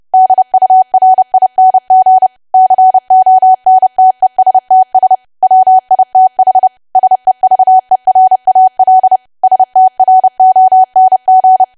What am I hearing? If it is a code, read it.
Morse code, heard as a keyed tone, about 30 wpm, 750 Hz. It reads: DURING CONTESTS WITH SEVERAL STRONG
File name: cwa / W1AW-Level3 / 30WPM